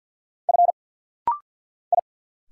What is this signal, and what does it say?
Morse code, heard as a keyed tone, about 50 wpm, 700 Hz. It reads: F I
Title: Single Letter-Number Alt - Rapid-Fire 50wpm